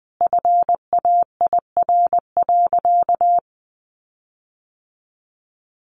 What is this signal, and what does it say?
Morse code, heard as a keyed tone, about 20 wpm, 700 Hz. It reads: FAIR.
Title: Sentences from Top 1300 Words 20wpm